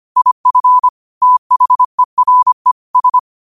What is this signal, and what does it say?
Morse code, heard as a keyed tone, about 25 wpm, 1000 Hz. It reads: IF THERE S
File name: qotdfiles / 25wpm